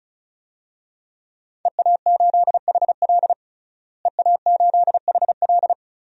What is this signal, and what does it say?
Morse code, heard as a keyed tone, about 34 wpm, 700 Hz. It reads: EA8HL EA8HL
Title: DXCC-100 Call Signs - 5 Characters - Repeated 2x - 35wpm